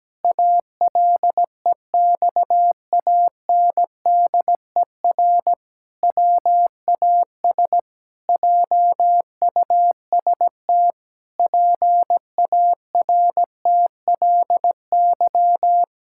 Morse code, heard as a keyed tone, about 17 wpm, 700 Hz. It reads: ALEXANDER WAS JUST PARTLY
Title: Pangram Challenge v2 - 17wpm